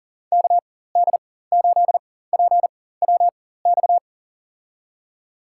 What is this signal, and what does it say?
Morse code, heard as a keyed tone, about 40 wpm, spaced as for 19 wpm, 700 Hz. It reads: KD8PWX
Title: USA Call Signs - 6 Characters - 4x Character Spacing - 40wpm